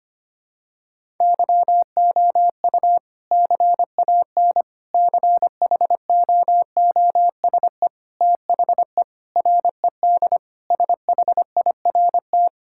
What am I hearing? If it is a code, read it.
Morse code, heard as a keyed tone, about 25 wpm, 700 Hz. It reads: YOU CAN CHOOSE THE RED SHIRT
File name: practice / 25wpm / Sentences with 2-Word Phrases - 25wpm